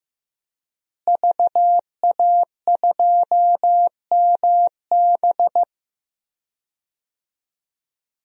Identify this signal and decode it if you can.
Morse code, heard as a keyed tone, about 15 wpm, 700 Hz. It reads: VA2MB
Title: DXCC-100 Call Signs - 5 Characters - Repeated 3x - 15wpm